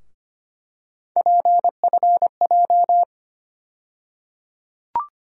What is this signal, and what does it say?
Morse code, heard as a keyed tone, about 25 wpm, 700 Hz. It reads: PFJ
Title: USA Call Sign Postfixes - 3 Letters 25wpm